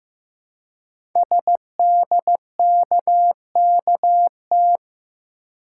Morse code, heard as a keyed tone, about 15 wpm, 700 Hz. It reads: SDKKT
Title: Code Groups - 5 Letters - 15wpm